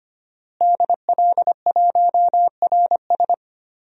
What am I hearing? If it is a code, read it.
Morse code, heard as a keyed tone, about 25 wpm, 700 Hz. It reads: DL1RS
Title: DXCC-100 Call Signs - 5 Characters - Repeated 3x - 25wpm